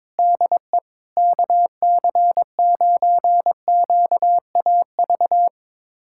Morse code, heard as a keyed tone, about 22 wpm, 700 Hz. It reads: DE KC9QAV
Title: USA Callsign DE Callsign - 6 Characters - 22wpm